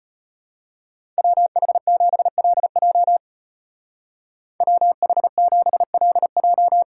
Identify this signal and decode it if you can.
Morse code, heard as a keyed tone, about 33 wpm, 700 Hz. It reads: WH7LJ WH7LJ